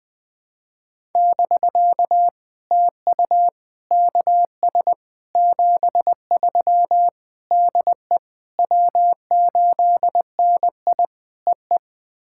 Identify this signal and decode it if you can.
Morse code, heard as a keyed tone, about 20 wpm, 700 Hz. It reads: <BK> TU KS 73 DE W8NI EE
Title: POTA QSO Practice - 20wpm